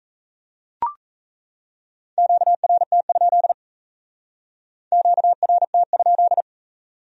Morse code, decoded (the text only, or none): QRT? QRT?